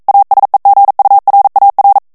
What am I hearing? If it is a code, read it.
Morse code, heard as a keyed tone, about 42 wpm, 800 Hz. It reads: ASEGURAR
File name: morse mp3 / ESPAGNOL TEXTE EN CLAIR 1 VITESSE 40 MOTS MINUTES